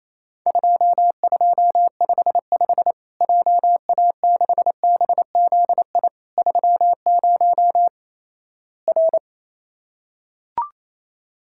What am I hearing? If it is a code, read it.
Morse code, heard as a keyed tone, about 28 wpm, 700 Hz. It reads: 2255 JA6BZI 30 R
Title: Worked All Europe DX Contest - QTC Exchange - 28wpm